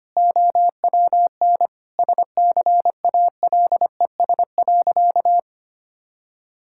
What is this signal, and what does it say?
Morse code, heard as a keyed tone, about 25 wpm, 700 Hz. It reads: OWN SCALES.
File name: practice / 25wpm / All-Age Appropriate Jokes - 25wpm